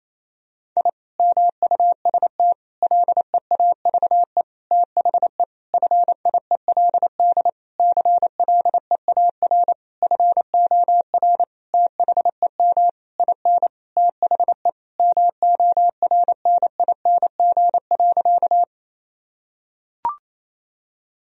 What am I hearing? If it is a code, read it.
Morse code, heard as a keyed tone, about 28 wpm, 700 Hz. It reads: I MUST LEAVE THE FIELD CLEAR FOR THEM IN THE MORNING.
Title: Sentences from Top 1000 Words - Review Part 2 - 28wpm